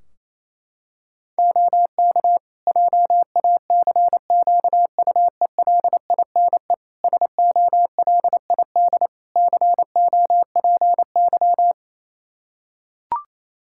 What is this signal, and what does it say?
Morse code, heard as a keyed tone, about 27 wpm, 700 Hz. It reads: OK JACQUELINE SOLID COPY
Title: K1USN SST QSO Practice 28wpm